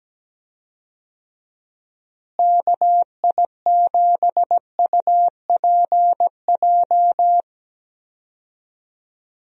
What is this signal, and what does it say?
Morse code, heard as a keyed tone, about 17 wpm, 700 Hz. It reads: KI7UPJ
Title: USA Call Signs - 6 Characters - Repeated 2x - 17wpm